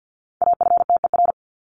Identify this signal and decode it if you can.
Morse code, heard as a keyed tone, about 48 wpm, 700 Hz. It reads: AFTER
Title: Top 200 Words 50wpm